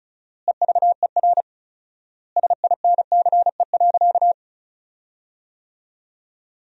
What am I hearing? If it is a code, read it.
Morse code, heard as a keyed tone, about 34 wpm, 700 Hz. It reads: EVER SINCE.
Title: 4x Word-Spacing - Sentences from Top 400 Words - 35wpm